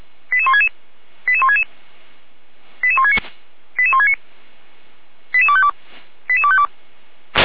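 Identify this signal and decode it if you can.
Five-tone selective call (ZVEI), heard as quick runs of five tones, tones of about 70 ms. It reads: 88166 88166 88169 88169 88252 88252
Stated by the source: ZVEI five-tone call received by SDR.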